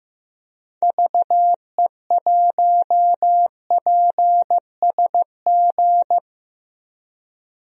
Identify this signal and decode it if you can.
Morse code, heard as a keyed tone, about 15 wpm, 700 Hz. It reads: VE1PSG
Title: DXCC-100 Call Signs - 6 Characters - Repeated 3x - 15wpm